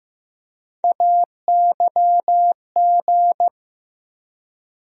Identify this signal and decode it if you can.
Morse code, heard as a keyed tone, about 15 wpm, 700 Hz. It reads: AYG
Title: USA Call Sign Postfixes - 3 Letters 15wpm